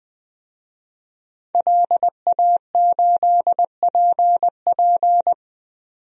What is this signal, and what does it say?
Morse code, heard as a keyed tone, about 20 wpm, 700 Hz. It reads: LA8PP
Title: DXCC-100 Call Signs - 5 Characters - 20wpm